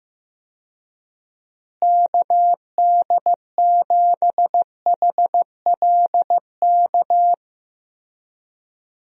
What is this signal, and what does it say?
Morse code, heard as a keyed tone, about 15 wpm, 700 Hz. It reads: KD7HLK
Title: USA Call Signs - 6 Characters - Repeated 2x - 15wpm